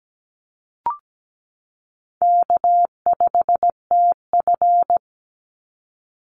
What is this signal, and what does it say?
Morse code, heard as a keyed tone, about 17 wpm, 700 Hz. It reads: K5TF
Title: Call Signs Repeated 2x - 17wpm